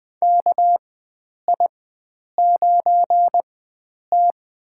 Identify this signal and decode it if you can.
Morse code, heard as a keyed tone, about 20 wpm, spaced as for 9 wpm, 700 Hz. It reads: KI9T
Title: USA Call Signs - 4 Characters - 4x Character Spacing - 20wpm